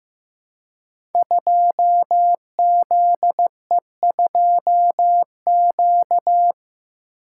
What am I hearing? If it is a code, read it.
Morse code, heard as a keyed tone, about 15 wpm, 700 Hz. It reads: 2ZE2Q
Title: Code Groups - 5 Letter:Numbers - 15wpm